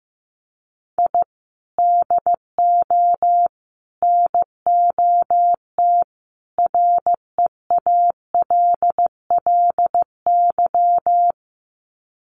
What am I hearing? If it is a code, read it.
Morse code, heard as a keyed tone, about 15 wpm, 700 Hz. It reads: I DO NOT REALLY
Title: Sets of 4 Words 15wpm